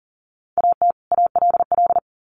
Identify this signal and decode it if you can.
Morse code, heard as a keyed tone, about 39 wpm, 700 Hz. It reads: AT ALL
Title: Sets of 2 Words 40wpm